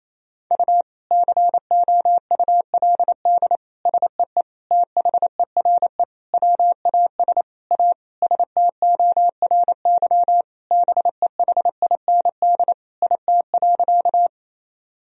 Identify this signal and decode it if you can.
Morse code, heard as a keyed tone, about 28 wpm, 700 Hz. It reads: U COULD SEE THERE WAS A STORY BEHIND IT.
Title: Sentences from Top 500 Words and Abbreviations 28wpm